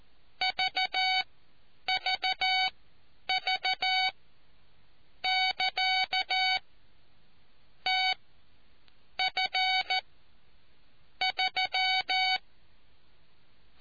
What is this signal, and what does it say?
Morse code, heard as a keyed tone, about 14 wpm, spaced as for 7 wpm, 750 Hz. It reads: VVV<KA> TF3